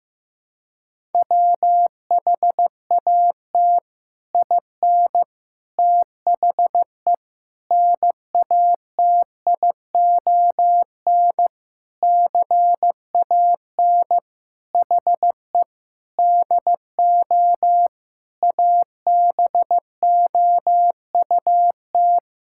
Morse code, heard as a keyed tone, about 15 wpm, 700 Hz. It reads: WHAT IN THE NATION CAN HE DO ABOUT
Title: Sentences from Top 1000 Words - Review Part 6 - 15wpm